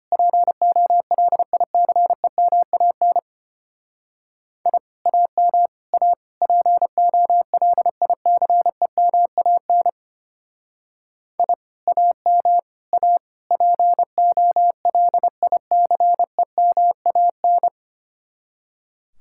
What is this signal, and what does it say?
Morse code, heard as a keyed tone, about 32 wpm, 700 Hz. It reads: POLICEMAN I AM A POLICEMAN I AM A POLICEMAN